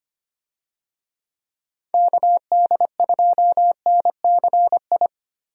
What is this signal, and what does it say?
Morse code, heard as a keyed tone, about 25 wpm, 700 Hz. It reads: KD2NCI
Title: USA Call Signs - 6 Characters - Repeated 2x - 25wpm